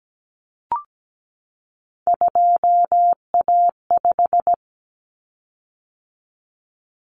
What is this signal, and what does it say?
Morse code, heard as a keyed tone, about 17 wpm, 700 Hz. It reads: 2A5
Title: Common Cut Numbers - 3-Digit 17wpm